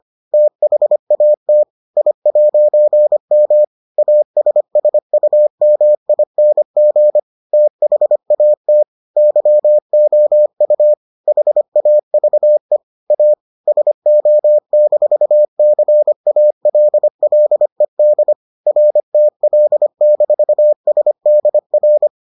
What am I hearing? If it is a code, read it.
Morse code, heard as a keyed tone, about 25 wpm, 600 Hz. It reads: THAT I'M ASSUMING THAT YOU HAVE A SO-CALLED RTL-SDR